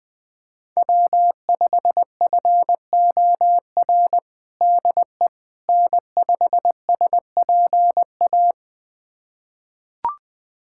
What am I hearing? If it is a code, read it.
Morse code, heard as a keyed tone, about 21 wpm, 700 Hz. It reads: W5FOR DE N5SPA
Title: USA Callsign DE Callsign - 5 Characters - 20wpm